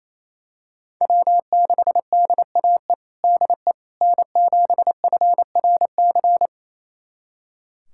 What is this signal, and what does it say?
Morse code, heard as a keyed tone, about 28 wpm, 700 Hz. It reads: W6DAE DE N7FRC
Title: USA Callsign DE Callsign - 5 Characters - 28wpm